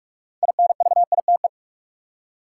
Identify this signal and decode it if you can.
Morse code, heard as a keyed tone, about 46 wpm, 700 Hz. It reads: INVITE
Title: Six Letter Words - 45wpm